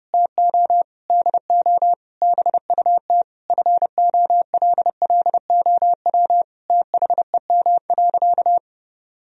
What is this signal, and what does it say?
Morse code, heard as a keyed tone, about 30 wpm, 700 Hz. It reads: TO DO BUT FOLLOW THEM.
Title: Sentences from Top 1000 Words - Review Part 1 - 30wpm